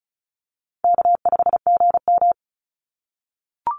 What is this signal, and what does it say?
Morse code, heard as a keyed tone, about 36 wpm, 700 Hz. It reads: K5GM
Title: Call Signs - 35wpm